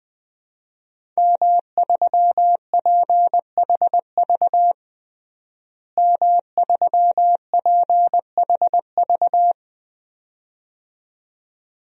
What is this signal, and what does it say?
Morse code, heard as a keyed tone, about 20 wpm, 700 Hz. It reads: M3PHV M3PHV
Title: DXCC-100 Call Signs - 5 Characters - Repeated 2x - 20wpm